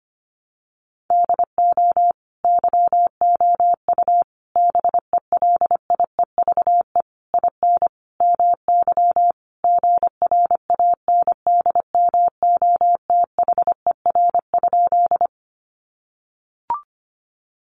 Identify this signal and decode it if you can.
Morse code, heard as a keyed tone, about 25 wpm, 700 Hz. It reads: DO YOU BELIEVE IN MY GRANDMOTHER?
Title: Sentences from Top 1200 Words 25wpm